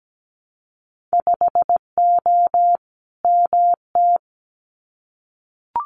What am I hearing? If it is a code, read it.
Morse code, heard as a keyed tone, about 17 wpm, 700 Hz. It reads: 5O MT
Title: Winter Field Day Exchanges Encore 17wpm